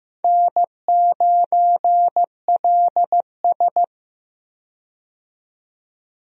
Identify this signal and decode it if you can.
Morse code, heard as a keyed tone, about 15 wpm, 700 Hz. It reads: N9LS
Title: USA Call Signs - 4 Characters - 15wpm